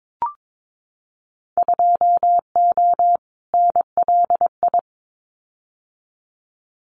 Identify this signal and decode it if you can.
Morse code, heard as a keyed tone, about 22 wpm, 700 Hz. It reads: 2O NLI
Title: Winter Field Day Exchanges 22wpm